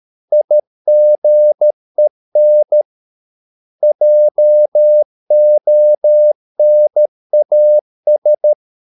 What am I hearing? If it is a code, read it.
Morse code, heard as a keyed tone, about 13 wpm, 600 Hz. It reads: IGEN JONAS
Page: Ferie 13 wpm